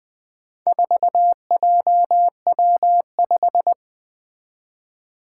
Morse code, heard as a keyed tone, about 21 wpm, 700 Hz. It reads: 4JW5
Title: Four Challenging Letter-Number Combinations 20wpm